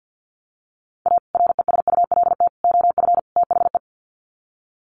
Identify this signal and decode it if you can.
Morse code, heard as a keyed tone, about 48 wpm, 700 Hz. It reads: A RESULT OF THE
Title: Sets of 4 Words 50wpm